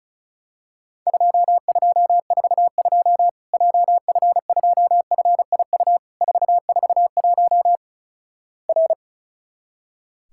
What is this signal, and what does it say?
Morse code, heard as a keyed tone, about 36 wpm, 700 Hz. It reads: 2242 JF2FIU 441 R
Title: Worked All Europe DX Contest - QTC Exchange - 35wpm